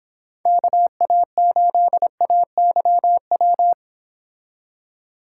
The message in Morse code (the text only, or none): KA8AYW